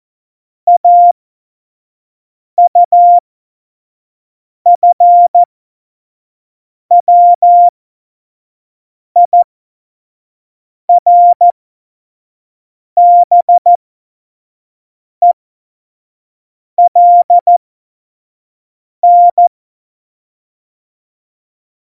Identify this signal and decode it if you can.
Morse code, heard as a keyed tone, about 14 wpm, spaced as for 5 wpm, 700 Hz. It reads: AUFWIRBELN
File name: Morsekurs / Witze / Witz 08. August 17, Tempo 25 F70